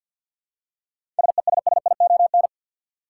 Morse code, heard as a keyed tone, about 50 wpm, 700 Hz. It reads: SESSION
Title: Seven Letter Words - 50wpm